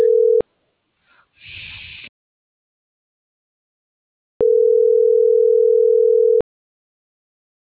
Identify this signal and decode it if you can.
Ringback tone (North America), 440 + 480 Hz.